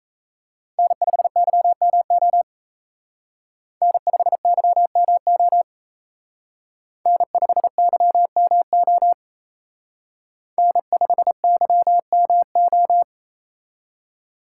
Morse code, heard as a keyed tone, about 33 wpm, 700 Hz. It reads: N5YMO N5YMO N5YMO N5YMO
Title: Speed Racing - USA 1x3 Call Signs 28wpm